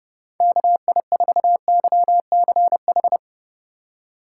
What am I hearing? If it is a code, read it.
Morse code, heard as a keyed tone, about 30 wpm, 700 Hz. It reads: KI4YCH